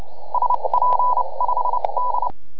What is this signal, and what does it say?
Morse code, heard as a keyed tone, about 29 wpm, spaced as for 24 wpm, 990 Hz, with noise. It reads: SP5D